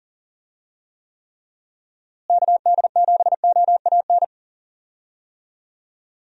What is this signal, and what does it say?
Morse code, heard as a keyed tone, about 40 wpm, 700 Hz. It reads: KD7OAN
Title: USA Call Signs - 6 Characters - Repeated 3x - 40wpm